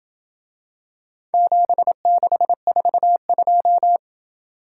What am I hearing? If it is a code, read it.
Morse code, heard as a keyed tone, about 27 wpm, 700 Hz. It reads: 7642